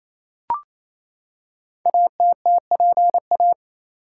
Morse code, heard as a keyed tone, about 29 wpm, 700 Hz. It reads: ATTPA
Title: Code Groups - 5 Letters - 28wpm